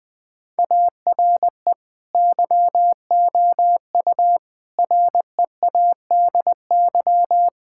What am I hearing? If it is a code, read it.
Morse code, heard as a keyed tone, about 21 wpm, 700 Hz. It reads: ARE YOU READY
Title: Sentences from Top 1000 Words - Review Part 6 - 20wpm